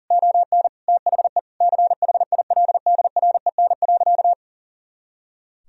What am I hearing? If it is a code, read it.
Morse code, heard as a keyed tone, about 40 wpm, 700 Hz. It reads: ON THE CHILDREN.
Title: Sentences from Top 300 Words and Abbreviations 40wpm